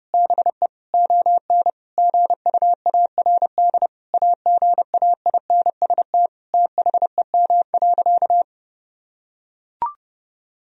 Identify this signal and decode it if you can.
Morse code, heard as a keyed tone, about 30 wpm, 700 Hz. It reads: BE ON GUARD AGAINST THEM.
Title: Sentences from Top 1000 Words - Review Part 4 - 30wpm